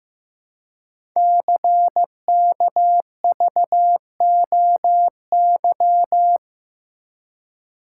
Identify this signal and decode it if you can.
Morse code, heard as a keyed tone, about 15 wpm, 700 Hz. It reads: CKVOY
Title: Code Groups - 5 Letters - 15wpm